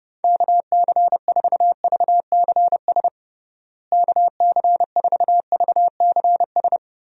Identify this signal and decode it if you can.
Morse code, heard as a keyed tone, about 30 wpm, 700 Hz. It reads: KC4VCS KC4VCS
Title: USA Call Signs - 6 Characters - Repeated 2x - 30wpm